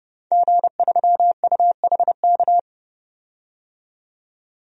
Morse code, heard as a keyed tone, about 32 wpm, 700 Hz. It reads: G3UHK
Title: DXCC-100 Call Signs - 5 Characters - 30wpm